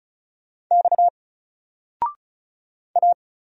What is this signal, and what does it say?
Morse code, heard as a keyed tone, about 35 wpm, 700 Hz. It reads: X A
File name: Single-Letter-Number-35wpm